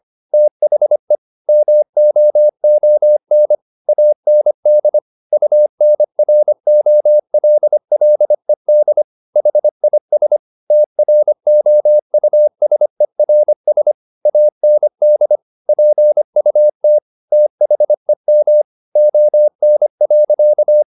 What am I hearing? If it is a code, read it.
Morse code, heard as a keyed tone, about 25 wpm, 600 Hz. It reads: THE MOON AND UNROLLED HIS TROUSERS AND PUT THEM ON.
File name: the old man and the sea part 1 25wpm